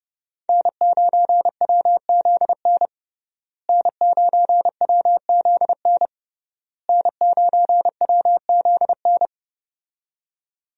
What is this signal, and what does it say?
Morse code, heard as a keyed tone, about 32 wpm, 700 Hz. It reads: N9WZN N9WZN N9WZN
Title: USA Call Signs - 5 Characters - Repeated 3x - 30wpm